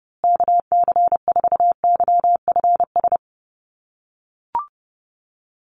Morse code, heard as a keyed tone, about 30 wpm, 700 Hz. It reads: KC4YFS
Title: USA Call Signs Repeated 3x 30wpm